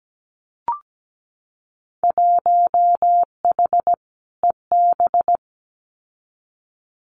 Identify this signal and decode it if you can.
Morse code, heard as a keyed tone, about 17 wpm, 700 Hz. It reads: 1H EB
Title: Winter Field Day Exchanges 17wpm